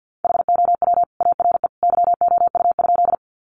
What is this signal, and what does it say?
Morse code, heard as a keyed tone, about 50 wpm, 700 Hz. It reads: HOW ARE YOU?
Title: Sentences from Top 100 Words 50wpm